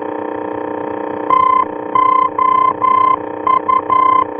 Morse code, heard as a keyed tone, about 11 wpm, 1030 Hz, with noise. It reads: TOU
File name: france / mx / TOU-F-VOR